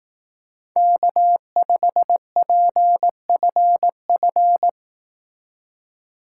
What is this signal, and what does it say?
Morse code, heard as a keyed tone, about 18 wpm, 700 Hz. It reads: K5PFF
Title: Speed Racing - USA Call Signs 15wpm